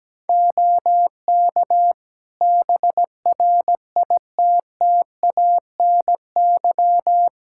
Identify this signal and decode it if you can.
Morse code, heard as a keyed tone, about 17 wpm, 700 Hz. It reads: OK BRITTANY
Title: K1USN SST QSO Practice 17wpm